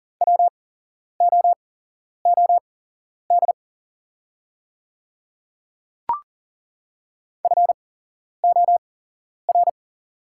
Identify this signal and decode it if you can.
Morse code, heard as a keyed tone, about 40 wpm, spaced as for 11 wpm, 690 Hz. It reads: WOOD FOR
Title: 8x Character Spacing - Top 300 Words - 40wpm